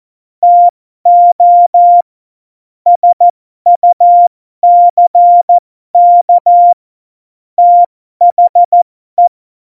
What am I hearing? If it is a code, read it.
Morse code, heard as a keyed tone, about 14 wpm, spaced as for 12 wpm, 700 Hz. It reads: TO SUCK THE